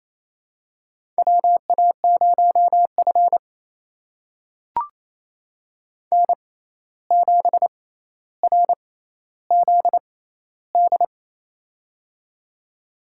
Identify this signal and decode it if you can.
Morse code, heard as a keyed tone, about 28 wpm, spaced as for 13 wpm, 700 Hz. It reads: WA0F N7RZD
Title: USA Call Signs - 6x Character Spacing - 28wpm